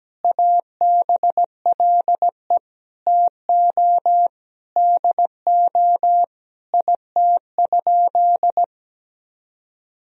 Morse code, heard as a keyed tone, about 17 wpm, 700 Hz. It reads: ABLE TO DO IT?
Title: Sentences from Top 1000 Words - Review Part 2 - 17wpm